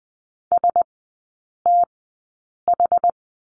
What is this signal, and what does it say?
Morse code, heard as a keyed tone, about 20 wpm, spaced as for 8 wpm, 700 Hz. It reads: STH
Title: 178 Lesson 29e - Slash - Three Characters 20wpm